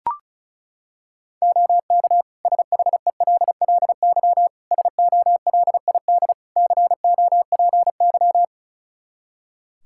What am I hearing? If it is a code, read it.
Morse code, heard as a keyed tone, about 34 wpm, 700 Hz. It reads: OK SHELLY SOLID COPY
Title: Practice QSO - Exchange 3 - More Info - 35wpm